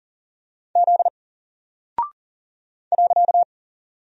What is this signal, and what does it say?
Morse code, heard as a keyed tone, about 40 wpm, 700 Hz. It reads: Z .
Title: Single Character (Letter:Number:Punctuation:Prosign) 40wpm